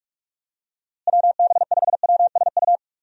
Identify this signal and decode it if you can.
Morse code, heard as a keyed tone, about 42 wpm, 700 Hz. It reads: WB5WSU
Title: USA Call Signs - 6 Characters - Repeated 2x - 45wpm